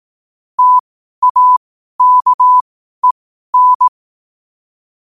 Morse code, heard as a keyed tone, about 18 wpm, spaced as for 12 wpm, 1000 Hz. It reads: TAKEN
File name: qotdfiles / 12wpm